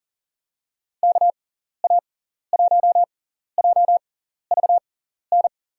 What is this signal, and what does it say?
Morse code, heard as a keyed tone, about 40 wpm, spaced as for 14 wpm, 700 Hz. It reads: KA1JVN
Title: USA Call Signs - 6 Characters - 6x Character Spacing - 40wpm